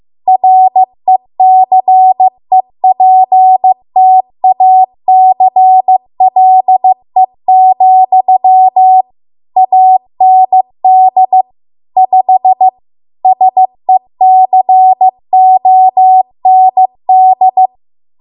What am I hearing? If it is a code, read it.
Morse code, heard as a keyed tone, about 15 wpm, 750 Hz. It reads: RECEPTACLE, AND 5 SECOND
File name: Archive / 15 WPM / 15WPM